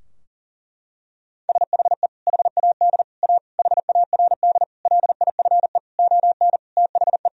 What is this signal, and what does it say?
Morse code, heard as a keyed tone, about 40 wpm, 700 Hz. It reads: SHE HAD A HARD LIFE ON THE
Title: Sentences from Top 1000 Words - Review Part 2 - 40wpm